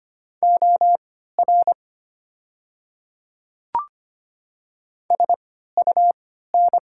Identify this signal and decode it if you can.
Morse code, heard as a keyed tone, about 25 wpm, spaced as for 14 wpm, 700 Hz. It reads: OR SUN
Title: 3x Character Spacing - Top 300 Words - 25wpm